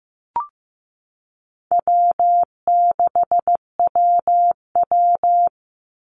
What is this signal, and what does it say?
Morse code, heard as a keyed tone, about 15 wpm, 700 Hz. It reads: W6WW
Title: NA SOTA Chaser Callsigns 15wpm